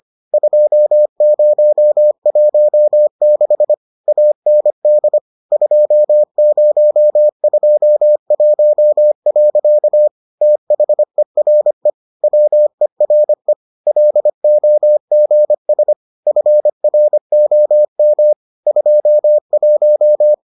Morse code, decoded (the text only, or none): 2016 AND 2021. THERE WERE LOGS FROM 21